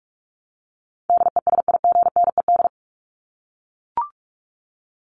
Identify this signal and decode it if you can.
Morse code, heard as a keyed tone, about 44 wpm, 700 Hz. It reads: DESIGNED
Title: ED Suffix - Top 100 Words with Base Words 45wpm